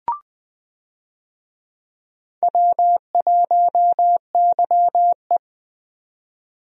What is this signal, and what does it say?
Morse code, heard as a keyed tone, about 20 wpm, 700 Hz. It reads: W1YE
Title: USA Call Signs - 4 Characters - 20wpm